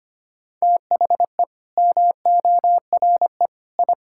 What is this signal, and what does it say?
Morse code, heard as a keyed tone, about 25 wpm, 700 Hz. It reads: THE MORE I
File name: Sets of 9 Words - 25wpm